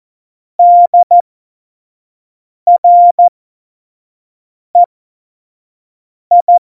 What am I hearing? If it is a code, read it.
Morse code, heard as a keyed tone, about 14 wpm, spaced as for 5 wpm, 700 Hz. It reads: DREI
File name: Morsekurs / Witze / Witz 15. Mai 2018, 25 F70